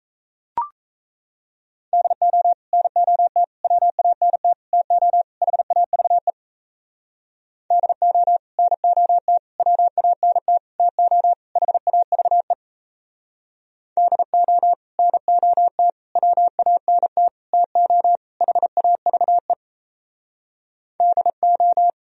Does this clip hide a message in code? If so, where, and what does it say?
Yes: Morse code from 1.9 to 21.9 s, heard as a keyed tone, about 36 wpm, 700 Hz. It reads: DO NOT WANT TO HAVE DO NOT WANT TO HAVE DO NOT WANT TO HAVE DO